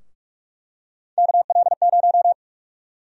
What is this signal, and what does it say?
Morse code, heard as a keyed tone, about 46 wpm, 700 Hz. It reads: KL0
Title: USA Call Sign Prefixes - 45wpm